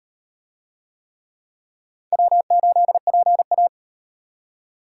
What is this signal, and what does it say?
Morse code, heard as a keyed tone, about 35 wpm, 700 Hz. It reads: W8PA